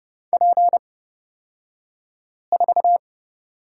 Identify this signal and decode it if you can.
Morse code, heard as a keyed tone, about 29 wpm, spaced as for 5 wpm, 700 Hz. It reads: P4
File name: Lesson 36d - BT - Two Characters 30wpm